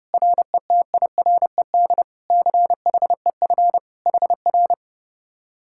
Morse code, heard as a keyed tone, about 33 wpm, 700 Hz. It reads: RETIRED CHEF HR